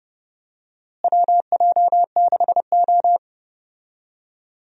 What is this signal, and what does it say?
Morse code, heard as a keyed tone, about 30 wpm, 700 Hz. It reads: WJ6O
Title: USA Call Signs - 4 Characters - Repeated 2x - 30wpm